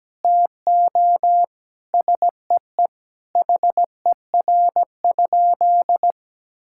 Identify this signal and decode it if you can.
Morse code, heard as a keyed tone, about 17 wpm, 700 Hz. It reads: TO SEE HER?
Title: Sentences from Top 1600 Words 17wpm